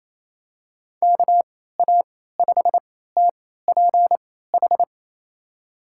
Morse code, heard as a keyed tone, about 28 wpm, spaced as for 16 wpm, 700 Hz. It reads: KA5TPH
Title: USA Call Signs - 6 Characters - 3x Character Spacing - 28wpm